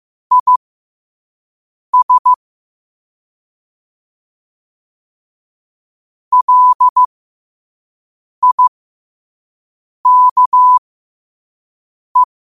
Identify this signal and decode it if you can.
Morse code, heard as a keyed tone, about 15 wpm, spaced as for 5 wpm, 1000 Hz. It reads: IS LIKE
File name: qotdfiles / 5wpm